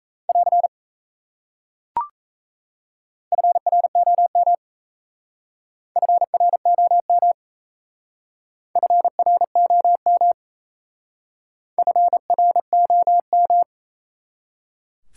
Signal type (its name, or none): Morse code